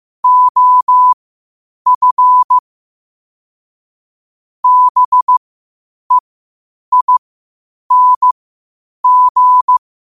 Morse code, heard as a keyed tone, about 15 wpm, spaced as for 8 wpm, 1000 Hz. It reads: OF BEING
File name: qotdfiles / 8wpm